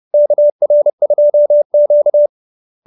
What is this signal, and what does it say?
Morse code, heard as a keyed tone, about 30 wpm, 590 Hz. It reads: KR2Q